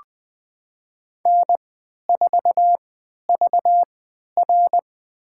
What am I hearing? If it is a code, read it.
Morse code, heard as a keyed tone, about 20 wpm, spaced as for 11 wpm, 700 Hz. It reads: N4VR
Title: USA Call Signs - 4 Characters - 3x Character Spacing - 20wpm